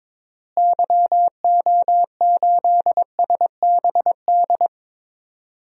Morse code, heard as a keyed tone, about 22 wpm, 700 Hz. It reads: YO8SBD
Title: DXCC-100 Call Signs - 6 Characters - Repeated 3x - 22wpm